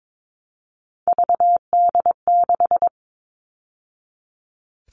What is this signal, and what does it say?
Morse code, heard as a keyed tone, about 22 wpm, 700 Hz. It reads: VD6
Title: All Cut Numbers - 3-Digit 22wpm